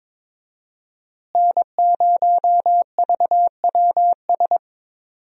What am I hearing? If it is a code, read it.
Morse code, heard as a keyed tone, about 22 wpm, 700 Hz. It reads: N0VWS
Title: USA Call Signs - 5 Characters - Repeated 3x - 22wpm